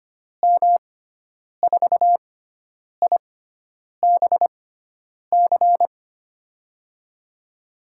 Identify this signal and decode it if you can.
Morse code, heard as a keyed tone, about 25 wpm, spaced as for 9 wpm, 700 Hz. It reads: M4IBC
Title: USA Call Signs - 6 Characters - 6x Character Spacing - 25wpm